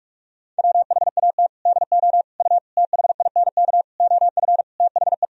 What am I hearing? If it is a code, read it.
Morse code, heard as a keyed tone, about 41 wpm, 700 Hz. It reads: WHAT DO U THINK OF THE
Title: Sentences from Top 300 Words and Abbreviations 45wpm